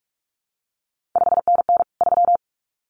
Morse code, heard as a keyed tone, about 46 wpm, 700 Hz. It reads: 5NN 3
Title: CQ WW DX Contest Exchange Encore - 45wpm